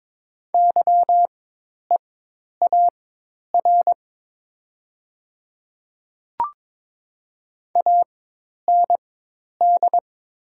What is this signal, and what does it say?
Morse code, heard as a keyed tone, about 21 wpm, spaced as for 10 wpm, 700 Hz. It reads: YEAR AND
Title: 4x Character Spacing - Top 200 Words - 22wpm